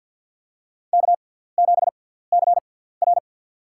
Morse code, heard as a keyed tone, about 53 wpm, spaced as for 17 wpm, 700 Hz. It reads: K7CR
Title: USA Call Signs - 4 Characters - 6x Character Spacing - 50wpm